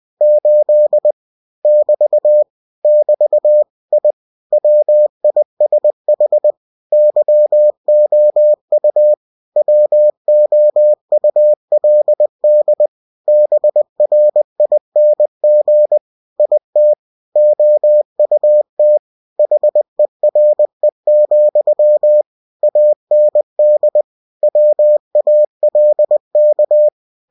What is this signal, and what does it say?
Morse code, heard as a keyed tone, about 20 wpm, 600 Hz. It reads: 8 <BT> <BT> I WISH YOU WOULD BRING IT OUT HERE, AND WALK